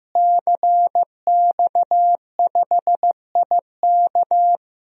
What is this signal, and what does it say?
Morse code, heard as a keyed tone, about 15 wpm, 700 Hz. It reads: CX5IK
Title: DXCC-100 Call Signs - 5 Characters - Repeated 2x - 15wpm